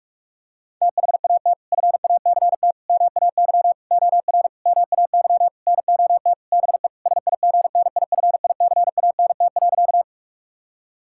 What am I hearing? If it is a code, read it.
Morse code, heard as a keyed tone, about 41 wpm, 700 Hz. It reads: THAT FACT MAY OR MAY NOT BE SIGNIFICANT.
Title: Sentences from Top 1000 Words - Review Part 6 - 45wpm